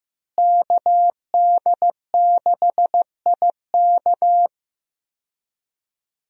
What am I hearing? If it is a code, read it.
Morse code, heard as a keyed tone, about 15 wpm, 700 Hz. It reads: KD6IK
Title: USA Call Signs - 5 Characters - Repeated 2x - 15wpm